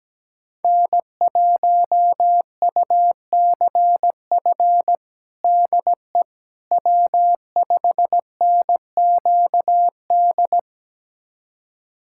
Morse code, heard as a keyed tone, about 17 wpm, 700 Hz. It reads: N1UCF DE W5NQD
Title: USA Callsign DE Callsign - 5 Characters - 17wpm